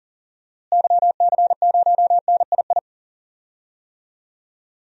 Morse code, heard as a keyed tone, about 39 wpm, 700 Hz. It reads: YC0NII
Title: DXCC-100 Call Signs - 6 Characters - Repeated 2x - 40wpm